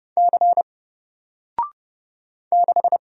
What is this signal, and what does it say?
Morse code, heard as a keyed tone, about 31 wpm, 700 Hz. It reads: C 6
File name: Single-Letter-Number-30wpm